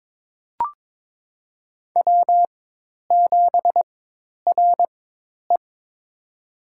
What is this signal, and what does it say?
Morse code, heard as a keyed tone, about 21 wpm, spaced as for 10 wpm, 700 Hz. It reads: W7RE
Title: USA Call Signs - 4 Characters - 4x Character Spacing - 22wpm